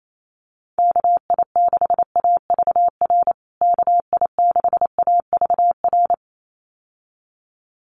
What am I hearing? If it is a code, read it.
Morse code, heard as a keyed tone, about 28 wpm, 700 Hz. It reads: KI6AVR KI6AVR
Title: USA Call Signs Repeated 2x 28wpm